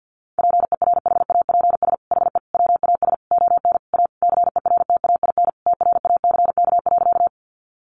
Morse code, heard as a keyed tone, about 48 wpm, 700 Hz. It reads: PERHAPS HE WAS ON A CERTAIN TRACK.
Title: Sentences from Top 1100 Words 50wpm